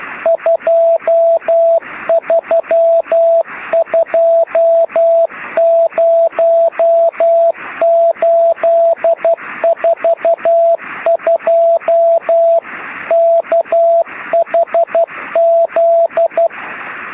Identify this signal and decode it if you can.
Morse code, heard as a keyed tone, about 12 wpm, 660 Hz, with radio noise. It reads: 2320842 KHZ
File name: sm7lcb / oh3shf 13cm